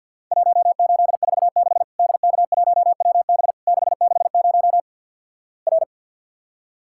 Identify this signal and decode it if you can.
Morse code, heard as a keyed tone, about 50 wpm, 700 Hz. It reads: 1846 DK1WB 660 R
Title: Worked All Europe DX Contest - QTC Exchange - 50wpm